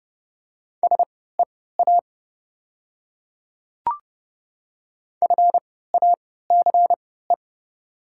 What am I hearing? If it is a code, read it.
Morse code, heard as a keyed tone, about 30 wpm, spaced as for 17 wpm, 690 Hz. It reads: SEA FACE
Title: 3x Character Spacing - Top 300 Words - 30wpm